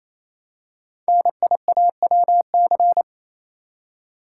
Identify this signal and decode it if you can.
Morse code, heard as a keyed tone, about 27 wpm, 700 Hz. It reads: NIAWC
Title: Code Groups - 5 Letters - 28wpm